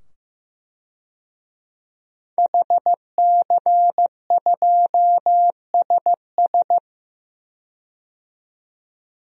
Morse code, heard as a keyed tone, about 15 wpm, 700 Hz. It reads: HC2SS
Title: DXCC-100 Call Signs - 5 Characters - 15wpm